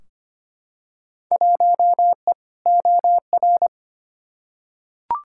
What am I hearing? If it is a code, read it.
Morse code, heard as a keyed tone, about 25 wpm, 700 Hz. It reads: 1E OR
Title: Field Day Exchanges - 25wpm